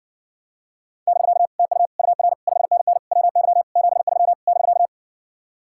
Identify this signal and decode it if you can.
Morse code, heard as a keyed tone, about 60 wpm, 700 Hz. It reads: <BK> TU UR 5NN WY 73 <BK>